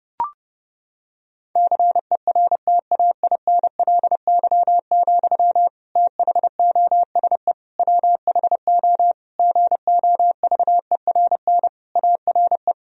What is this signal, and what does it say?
Morse code, heard as a keyed tone, about 30 wpm, 700 Hz. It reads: CERTAINLY, THOSE WHO GOVERN ARE
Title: Sentences from Top 1000 Words - Review Part 2 - 30wpm